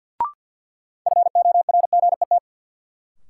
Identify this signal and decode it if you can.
Morse code, heard as a keyed tone, about 53 wpm, 700 Hz. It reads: FORGET
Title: Sentences from Top 1200 Words - New Word Drill - 50wpm